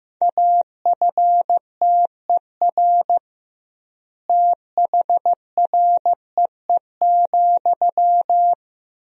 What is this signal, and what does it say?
Morse code, heard as a keyed tone, about 15 wpm, 700 Hz. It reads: AFTER THREE,
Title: 2x Word-Spacing - Sentences from Top 200 Words - 15wpm